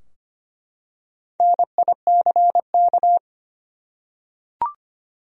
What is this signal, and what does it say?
Morse code, heard as a keyed tone, about 25 wpm, 700 Hz. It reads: NICK
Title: USA Common Names - 25wpm